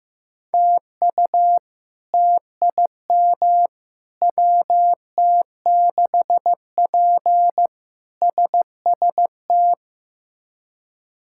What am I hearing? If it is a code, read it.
Morse code, heard as a keyed tone, about 15 wpm, 700 Hz. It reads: TU TIM WT6P SST
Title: K1USN SST QSO Practice - 15wpm